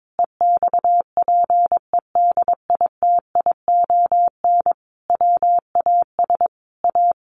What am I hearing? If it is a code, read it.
Morse code, heard as a keyed tone, about 22 wpm, 700 Hz. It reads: EXPEDITION WAS A
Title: A Sound Of Thunder 22wpm